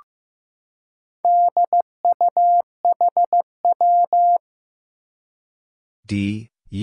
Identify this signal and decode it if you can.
Morse code, heard as a keyed tone, about 15 wpm, 700 Hz. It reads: DUHW
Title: Four Challenging Letter-Number Combinations 15wpm